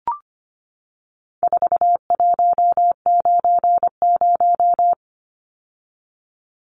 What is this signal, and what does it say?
Morse code, heard as a keyed tone, about 25 wpm, 700 Hz. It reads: 4190
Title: Numbers - 4-Digit 25wpm